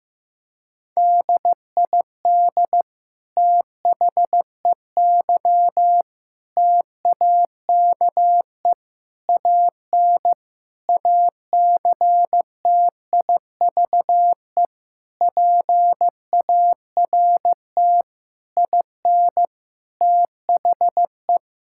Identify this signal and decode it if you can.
Morse code, heard as a keyed tone, about 15 wpm, 700 Hz. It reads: DID THEY TAKE AN ACTIVE PART IN THE
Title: Sentences from Top 1000 Words - Review Part 6 - 15wpm